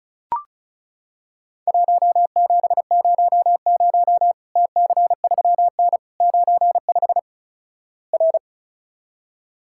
Morse code, heard as a keyed tone, about 36 wpm, 700 Hz. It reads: 1700 TC3N 95 R
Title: Worked All Europe DX Contest - QTC Exchange - 35wpm